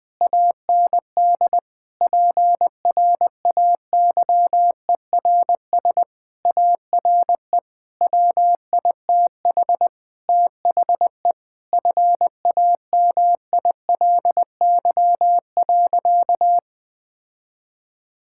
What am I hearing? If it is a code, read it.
Morse code, heard as a keyed tone, about 20 wpm, 700 Hz. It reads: AND PRAYERS ARE WITH THE FAMILY.
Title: Sentences with 7-Word Phrases - 20wpm